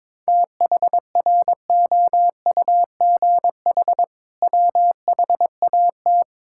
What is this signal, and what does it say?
Morse code, heard as a keyed tone, about 22 wpm, 700 Hz. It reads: THROUGH WHAT
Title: Sentences from Top 1000 Words - Review Part 6 - 22wpm